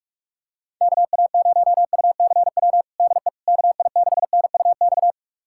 Morse code, heard as a keyed tone, about 41 wpm, 700 Hz. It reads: KA0UCW DE KI6NUX
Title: USA Callsign DE Callsign - 6 Characters - 45wpm